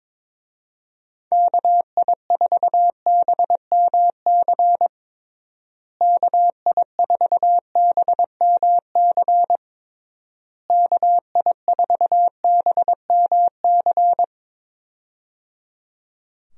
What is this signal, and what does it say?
Morse code, heard as a keyed tone, about 22 wpm, 700 Hz. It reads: KI4BMC KI4BMC KI4BMC
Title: USA Call Signs - 6 Characters - Repeated 3x - 22wpm